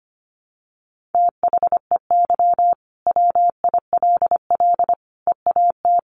Morse code, heard as a keyed tone, about 25 wpm, 700 Hz. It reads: THEY WILL EAT
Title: Sentences from Top 1000 Words 25wpm